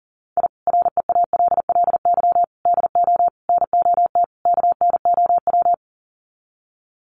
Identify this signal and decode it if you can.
Morse code, heard as a keyed tone, about 40 wpm, 700 Hz. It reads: I REALLY DO NOT KNOW
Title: Sets of 5 Words Encore 40wpm